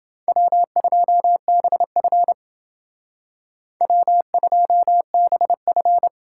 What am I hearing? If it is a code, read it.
Morse code, heard as a keyed tone, about 30 wpm, 700 Hz. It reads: W2BF W2BF